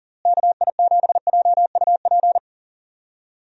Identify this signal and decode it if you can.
Morse code, heard as a keyed tone, about 40 wpm, 700 Hz. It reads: KI7JUP
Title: USA Call Signs - 6 Characters - Repeated 2x - 40wpm